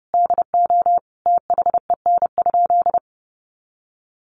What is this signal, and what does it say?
Morse code, heard as a keyed tone, about 30 wpm, 700 Hz. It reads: DO THEN?
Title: Sentences from Top 100 Words 30wpm